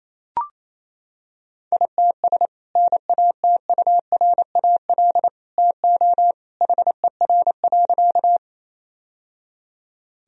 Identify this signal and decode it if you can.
Morse code, heard as a keyed tone, about 28 wpm, 700 Hz. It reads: ITS NATURAL TO HER.
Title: Sentences from Top 1000 Words - Review Part 4 - 28wpm